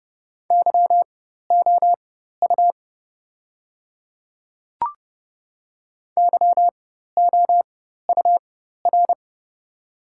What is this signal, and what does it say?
Morse code, heard as a keyed tone, about 30 wpm, spaced as for 14 wpm, 700 Hz. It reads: YOU YOUR
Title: 4x Character Spacing - Top 500 Words - 30wpm